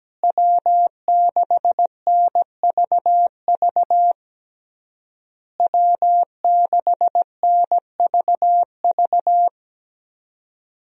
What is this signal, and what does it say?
Morse code, heard as a keyed tone, about 17 wpm, 700 Hz. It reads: W6NVV W6NVV
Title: USA Call Signs - 5 Characters - Repeated 3x - 17wpm